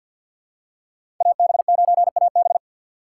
Morse code, heard as a keyed tone, about 47 wpm, 700 Hz. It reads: AB9AB
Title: USA Call Signs - 5 Characters - 3x Character Spacing - 50wpm